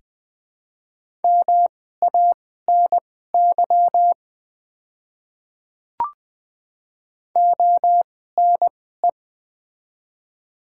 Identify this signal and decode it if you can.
Morse code, heard as a keyed tone, about 20 wpm, spaced as for 15 wpm, 700 Hz. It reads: MANY ONE
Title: 2x Character Spacing - Top 100 Words - 20wpm